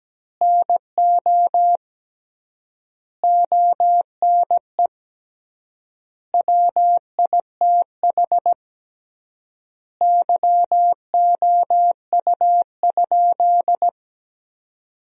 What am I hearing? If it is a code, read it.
Morse code, heard as a keyed tone, about 17 wpm, 700 Hz. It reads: NO ONE WITH YOU?
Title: 3x Word-Spacing - Sentences from Top 500 Words - 17wpm